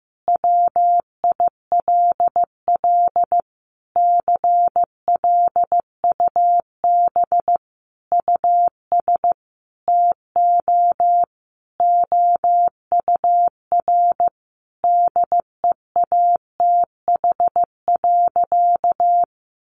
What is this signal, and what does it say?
Morse code, heard as a keyed tone, about 15 wpm, 700 Hz. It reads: WILL CLUB US TO OUR DEATH.
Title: Sentences from Top 1200 Words 15wpm